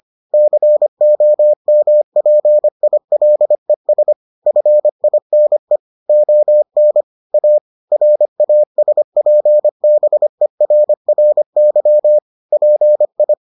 Morse code, heard as a keyed tone, about 25 wpm, 600 Hz. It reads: COMPILES FINE ON A RASPBERRY PI